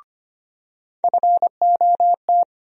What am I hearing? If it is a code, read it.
Morse code, heard as a keyed tone, about 25 wpm, 700 Hz. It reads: FOT
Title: USA Call Sign Postfixes - 3 Letters 25wpm